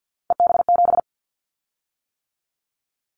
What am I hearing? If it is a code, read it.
Morse code, heard as a keyed tone, about 50 wpm, 700 Hz. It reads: EB7
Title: Callsign Prefixes for Top 25 DXCC - 50wpm